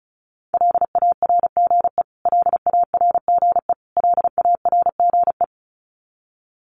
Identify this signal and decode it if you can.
Morse code, heard as a keyed tone, about 34 wpm, 700 Hz. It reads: LARGE LARGE LARGE
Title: Top 200 Words Repeated 35wpm